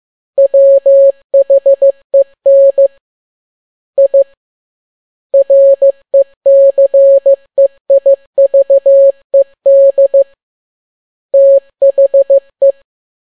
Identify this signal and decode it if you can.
Morse code, heard as a keyed tone, about 15 wpm, 550 Hz. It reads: WHEN I RECEIVED THE